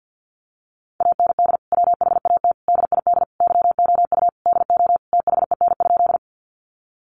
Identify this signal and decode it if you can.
Morse code, heard as a keyed tone, about 50 wpm, 700 Hz. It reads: AND WHAT DID YOU DO THEN?
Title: Sentences from Top 100 Words 50wpm